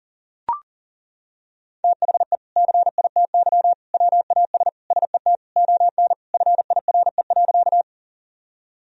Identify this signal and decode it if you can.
Morse code, heard as a keyed tone, about 40 wpm, 700 Hz. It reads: THE CITY WAS SET ON FIRE.
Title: Sentences from Top 1000 Words - Review Part 1 - 40wpm